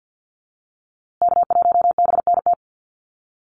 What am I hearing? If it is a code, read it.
Morse code, heard as a keyed tone, about 50 wpm, 700 Hz. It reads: K1BNT
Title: USA Call Signs Repeated 3x 50wpm